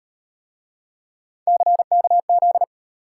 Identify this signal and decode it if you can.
Morse code, heard as a keyed tone, about 34 wpm, 700 Hz. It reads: CKZ